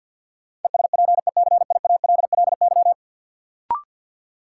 Morse code, heard as a keyed tone, about 51 wpm, 700 Hz. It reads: ESPECIALLY
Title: Sentences from Top 700 Words - New Word Drill - 50wpm